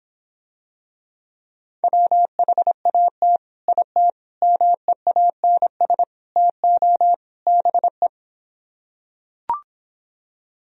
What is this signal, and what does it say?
Morse code, heard as a keyed tone, about 26 wpm, 700 Hz. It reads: WHAT IT MEANS TO BE